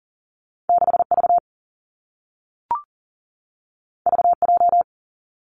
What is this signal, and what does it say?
Morse code, heard as a keyed tone, about 40 wpm, 700 Hz. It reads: 6 V V J
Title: Two Challenging Letter-Number Combinations 40wpm